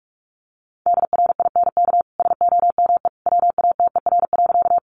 Morse code, heard as a keyed tone, about 44 wpm, 700 Hz. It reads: DRINK SOME WATER.
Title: Sentences from Top 1500 Words 45wpm